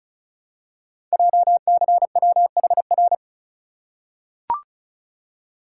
Morse code, heard as a keyed tone, about 36 wpm, 690 Hz. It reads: JCWHR
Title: Code Groups - 5 Letters - 35wpm